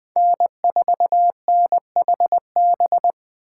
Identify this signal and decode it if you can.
Morse code, heard as a keyed tone, about 20 wpm, 700 Hz. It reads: N4NHB
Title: USA Call Signs - 5 Characters - 8x Character Spacing - 20wpm